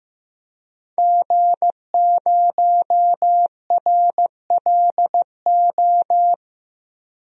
Morse code, heard as a keyed tone, about 15 wpm, 700 Hz. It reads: G0RLO
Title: DXCC-100 Call Signs - 5 Characters - Repeated 3x - 15wpm